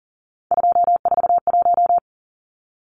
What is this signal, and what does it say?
Morse code, heard as a keyed tone, about 40 wpm, 700 Hz. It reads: 241
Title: Numbers - 3-Digit 40wpm